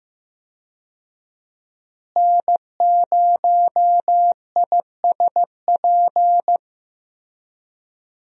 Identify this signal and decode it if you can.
Morse code, heard as a keyed tone, about 15 wpm, 700 Hz. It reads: N0ISP
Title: USA Call Signs - 5 Characters - Repeated 3x - 15wpm